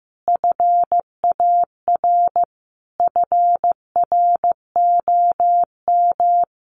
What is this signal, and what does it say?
Morse code, heard as a keyed tone, about 15 wpm, 700 Hz. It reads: FAR FROM
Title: Sentences from Top 1100 Words 15wpm